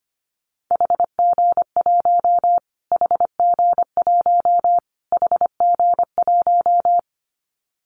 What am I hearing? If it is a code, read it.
Morse code, heard as a keyed tone, about 25 wpm, 700 Hz. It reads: HG1 HG1 HG1
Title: Callsign Prefixes for Top 25 DXCC - Repeated 3x - 25wpm